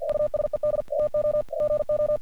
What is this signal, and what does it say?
Morse code, heard as a keyed tone, about 48 wpm, 600 Hz. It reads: USED TO GO
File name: archived / CRSnet